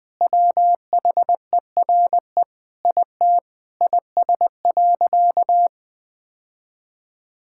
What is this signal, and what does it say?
Morse code, heard as a keyed tone, about 20 wpm, 700 Hz. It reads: WHERE IT IS.
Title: Sentences with 4-Word Phrases - 20wpm